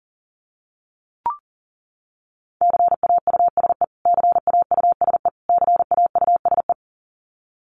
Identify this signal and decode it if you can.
Morse code, heard as a keyed tone, about 37 wpm, 700 Hz. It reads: CAUSE CAUSE CAUSE
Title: Top 300 Words Repeated 40wpm